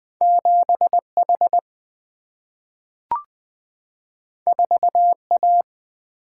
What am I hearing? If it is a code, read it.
Morse code, heard as a keyed tone, about 20 wpm, 700 Hz. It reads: 7H 4A
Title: Single-Letter Single-Number 20wpm